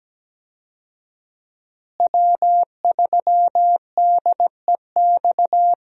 Morse code, heard as a keyed tone, about 17 wpm, 700 Hz. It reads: W3DEX
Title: USA Call Signs - 5 Characters - Repeated 2x - 17wpm